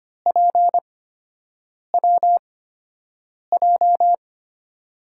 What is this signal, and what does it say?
Morse code, heard as a keyed tone, about 25 wpm, spaced as for 7 wpm, 700 Hz. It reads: PWJ
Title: USA Call Signs - 5 Characters - 8x Character Spacing - 25wpm